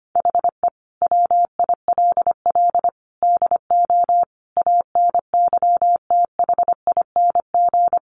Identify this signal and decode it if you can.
Morse code, heard as a keyed tone, about 25 wpm, 700 Hz. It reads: HE WILL DO ANYTHING
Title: Sentences from Top 1000 Words 25wpm